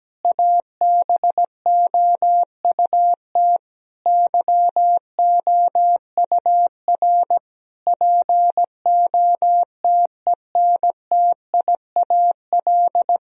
Morse code, heard as a keyed tone, about 17 wpm, 700 Hz. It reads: ABOUT YOUR POTENTIAL
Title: Sentences from Top 1000 Words - Review Part 4 - 17wpm